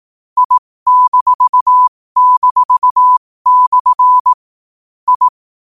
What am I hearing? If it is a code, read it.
Morse code, heard as a keyed tone, about 18 wpm, spaced as for 15 wpm, 1000 Hz. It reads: I--/ I
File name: qotdfiles / 15wpm